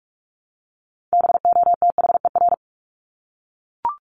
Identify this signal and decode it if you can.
Morse code, heard as a keyed tone, about 44 wpm, 700 Hz. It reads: BOTHER
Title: Top 1501-1750 Words 45wpm